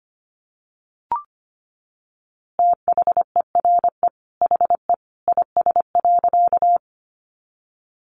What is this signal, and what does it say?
Morse code, heard as a keyed tone, about 25 wpm, 700 Hz. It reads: THERE HE IS.
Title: A Sound Of Thunder 25wpm